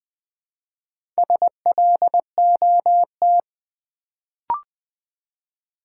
Morse code, heard as a keyed tone, about 20 wpm, 700 Hz. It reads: SLOT
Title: Build A Word - 1 Letter At A Time - 4-Letter Words - 20wpm